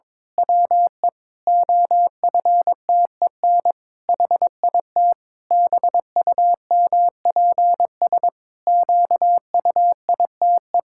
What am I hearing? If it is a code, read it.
Morse code, heard as a keyed tone, about 22 wpm, 700 Hz. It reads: WE OFTEN HIT BUMPS QUITE
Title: Pangram Challenge v2 - 22wpm